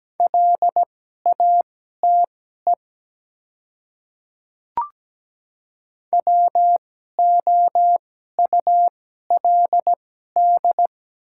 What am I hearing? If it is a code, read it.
Morse code, heard as a keyed tone, about 17 wpm, spaced as for 12 wpm, 690 Hz. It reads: LATE WOULD
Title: 2x Character Spacing - Top 300 Words - 17wpm